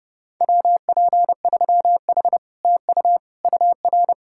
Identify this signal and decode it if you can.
Morse code, heard as a keyed tone, about 40 wpm, 700 Hz. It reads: WP3H TU UR